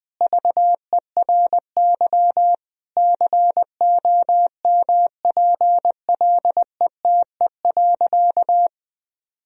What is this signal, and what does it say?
Morse code, heard as a keyed tone, about 21 wpm, 700 Hz. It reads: VERY COMPLETE.
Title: Sentences from Top 1000 Words - Review Part 2 - 20wpm